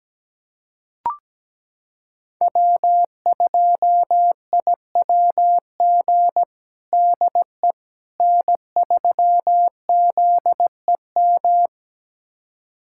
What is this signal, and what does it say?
Morse code, heard as a keyed tone, about 17 wpm, 700 Hz. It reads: W2IWG DE N3ZEM
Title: USA Callsign DE Callsign - 5 Characters - 17wpm